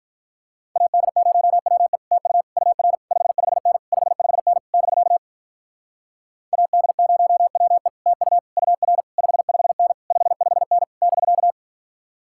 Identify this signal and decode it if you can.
Morse code, heard as a keyed tone, about 48 wpm, 700 Hz. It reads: AD0WE TU UR 55N 55N <BK> AD0WE TU UR 55N 55N <BK>